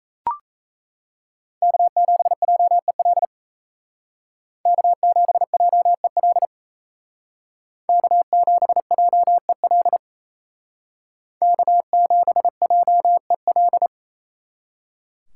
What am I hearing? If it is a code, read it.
Morse code, heard as a keyed tone, about 36 wpm, 700 Hz. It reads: K7JEL K7JEL K7JEL K7JEL